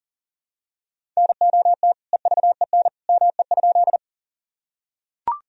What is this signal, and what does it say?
Morse code, heard as a keyed tone, about 39 wpm, 700 Hz. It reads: NOT EVEN ME?
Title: Sentences from Top 1000 Words - Review Part 1 - 40wpm